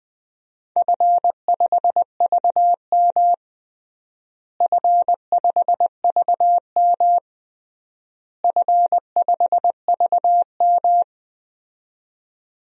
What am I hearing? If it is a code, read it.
Morse code, heard as a keyed tone, about 20 wpm, 700 Hz. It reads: F5VM F5VM F5VM
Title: DXCC-100 Call Signs - 4 Characters - Repeated 3x - 20wpm